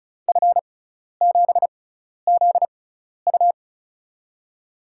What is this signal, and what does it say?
Morse code, heard as a keyed tone, about 34 wpm, spaced as for 12 wpm, 700 Hz. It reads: F7ZU
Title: USA Call Signs - 5 Characters - 6x Character Spacing - 35wpm